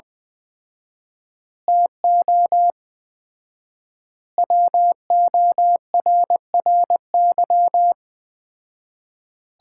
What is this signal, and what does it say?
Morse code, heard as a keyed tone, about 21 wpm, 700 Hz. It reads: TO WORRY
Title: 4x Word-Spacing - Sets of 5 Words - 20wpm